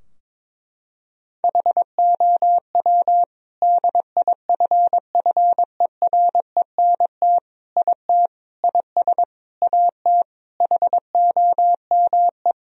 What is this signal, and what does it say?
Morse code, heard as a keyed tone, about 22 wpm, 700 Hz. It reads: HOW DIFFERENT IT IS AT HOME
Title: Sentences from Top 1000 Words - Review Part 3 - 22wpm